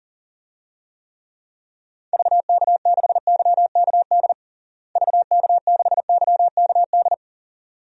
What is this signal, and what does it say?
Morse code, heard as a keyed tone, about 37 wpm, 700 Hz. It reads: VK6YKD VK6YKD
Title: DXCC-100 Call Signs - 6 Characters - Repeated 2x - 40wpm